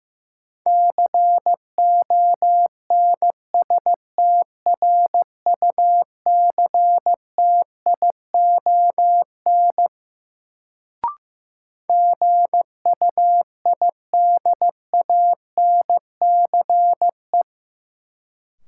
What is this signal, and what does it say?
Morse code, heard as a keyed tone, about 15 wpm, 700 Hz. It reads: CONSTRUCTION GUIDANCE
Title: Sentences from Top 1600 Words - New Word Drill Alt - 15wpm